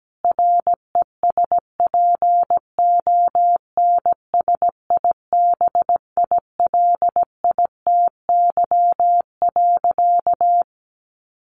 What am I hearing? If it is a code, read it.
Morse code, heard as a keyed tone, about 17 wpm, 700 Hz. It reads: RESPONSIBILITY.
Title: Sentences from Top 1500 Words 17wpm